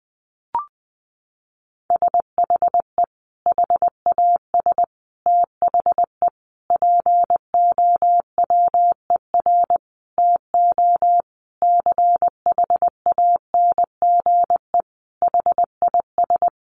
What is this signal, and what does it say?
Morse code, heard as a keyed tone, about 20 wpm, 700 Hz. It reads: SHE HAS THE POWER TO CHANGE HIS
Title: Sentences from Top 1200 Words 20wpm